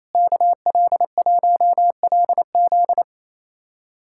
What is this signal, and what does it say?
Morse code, heard as a keyed tone, about 28 wpm, 700 Hz. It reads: KL1LZ
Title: USA Callsign DE Callsign - 5 Characters - 28wpm